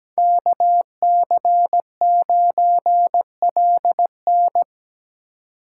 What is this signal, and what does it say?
Morse code, heard as a keyed tone, about 17 wpm, 700 Hz. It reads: KC9LN
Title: USA Call Signs - 5 Characters - 3x Character Spacing - 17wpm